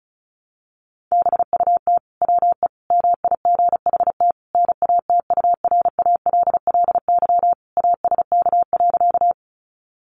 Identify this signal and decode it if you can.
Morse code, heard as a keyed tone, about 34 wpm, 700 Hz. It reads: BUT WE MIGHT NATURALLY ASK.
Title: Sentences from Top 1300 Words 35wpm